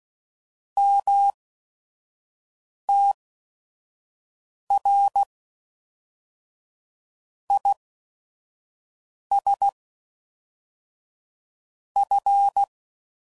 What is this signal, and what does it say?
Morse code, heard as a keyed tone, about 16 wpm, spaced as for 5 wpm, 800 Hz. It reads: MTR IS F
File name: CW Code Groups @5WPM / QSOs / QSO WA7QOX 1